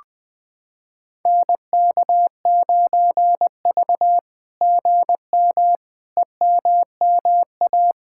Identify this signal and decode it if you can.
Morse code, heard as a keyed tone, about 20 wpm, 700 Hz. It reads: NK9V GM EMMA
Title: SOTA QSO Practice 20wpm